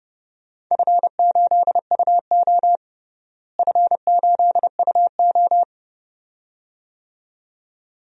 Morse code, heard as a keyed tone, about 30 wpm, 700 Hz. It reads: F8UO F8UO
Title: DXCC-100 Call Signs - 4 Characters - Repeated 2x - 30wpm